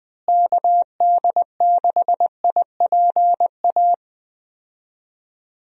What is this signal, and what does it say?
Morse code, heard as a keyed tone, about 20 wpm, 700 Hz. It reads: KD6IPA
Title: USA Call Signs - 6 Characters - Repeated 2x - 20wpm